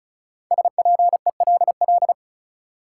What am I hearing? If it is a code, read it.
Morse code, heard as a keyed tone, about 36 wpm, 700 Hz. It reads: SPELL
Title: Sentences from Top 1000 Words - Review Part 1 - New Word Drill - 35wpm